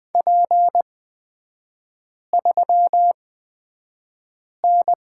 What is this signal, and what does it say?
Morse code, heard as a keyed tone, about 20 wpm, spaced as for 5 wpm, 700 Hz. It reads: P3N
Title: 250 Lesson 40e - BK - Three Characters 20wpm